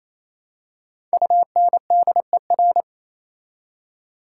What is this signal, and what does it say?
Morse code, heard as a keyed tone, about 28 wpm, 700 Hz. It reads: UNDER
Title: Prefixes and Suffixes 28wpm